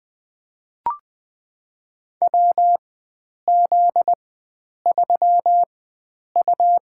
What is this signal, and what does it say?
Morse code, heard as a keyed tone, about 21 wpm, spaced as for 9 wpm, 700 Hz. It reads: WZ3U
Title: USA Call Signs - 4 Characters - 4x Character Spacing - 20wpm